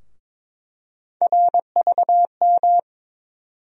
Morse code, heard as a keyed tone, about 22 wpm, 700 Hz. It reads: RVM
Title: USA Call Sign Postfixes - 3 Letters 22wpm